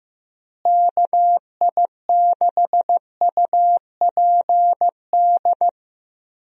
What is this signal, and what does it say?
Morse code, heard as a keyed tone, about 15 wpm, 700 Hz. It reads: KI6UPD
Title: USA Call Signs - 6 Characters - Repeated 3x - 15wpm